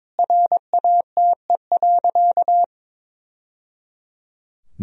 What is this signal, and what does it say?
Morse code, heard as a keyed tone, about 22 wpm, 700 Hz. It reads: RATE.
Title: Sentences from Top 1000 Words - Review Part 4 - 22wpm